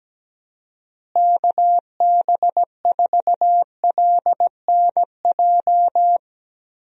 Morse code, heard as a keyed tone, about 17 wpm, 700 Hz. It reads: KB4LNJ
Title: USA Call Signs - 6 Characters - Repeated 3x - 17wpm